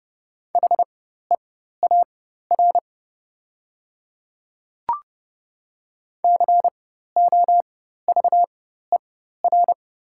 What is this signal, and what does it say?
Morse code, heard as a keyed tone, about 30 wpm, spaced as for 14 wpm, 700 Hz. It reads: HEAR COVER
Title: 4x Character Spacing - Top 500 Words - 30wpm